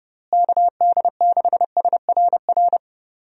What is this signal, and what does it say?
Morse code, heard as a keyed tone, about 30 wpm, 700 Hz. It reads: KD6SRR
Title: USA Call Signs - 6 Characters - Repeated 3x - 30wpm